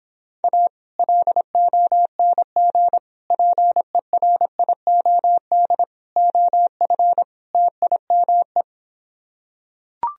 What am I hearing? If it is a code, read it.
Morse code, heard as a keyed tone, about 26 wpm, 700 Hz. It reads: A LONG PERIOD OF TIME